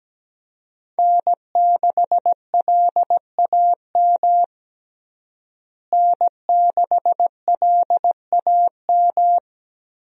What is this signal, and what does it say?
Morse code, heard as a keyed tone, about 17 wpm, 700 Hz. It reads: N6LAM N6LAM
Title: USA Call Signs - 5 Characters - Repeated 3x - 17wpm